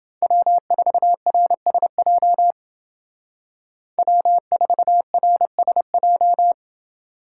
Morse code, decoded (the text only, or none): W4RSJ W4RSJ